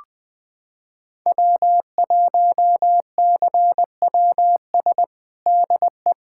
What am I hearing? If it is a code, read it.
Morse code, heard as a keyed tone, about 20 wpm, 700 Hz. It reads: W1CWS DE
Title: USA Callsign DE Callsign - 5 Characters - 20wpm